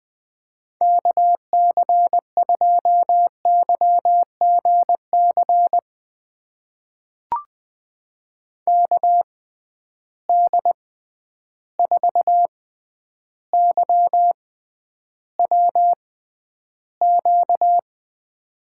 Morse code, heard as a keyed tone, about 20 wpm, spaced as for 10 wpm, 700 Hz. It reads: KC2YGC KD4YWQ
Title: USA Call Signs - 6 Characters - 6x Character Spacing - 20wpm